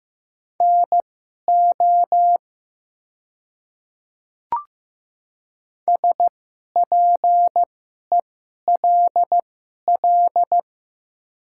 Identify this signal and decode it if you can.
Morse code, heard as a keyed tone, about 15 wpm, spaced as for 11 wpm, 700 Hz. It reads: NO SPELL
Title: 2x Character Spacing - Top 500 Words - 15wpm